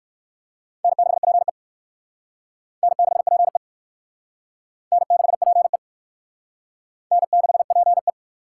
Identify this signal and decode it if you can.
Morse code, heard as a keyed tone, about 56 wpm, 700 Hz. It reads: N6PE N6PE N6PE N6PE
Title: Speed Racing - USA 1x2 Call Signs 45wpm